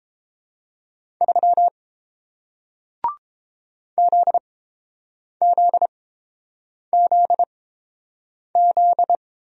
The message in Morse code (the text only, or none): 3 ZZZZ